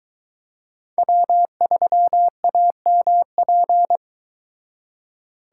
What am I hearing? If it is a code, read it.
Morse code, heard as a keyed tone, about 23 wpm, 700 Hz. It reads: W3AMP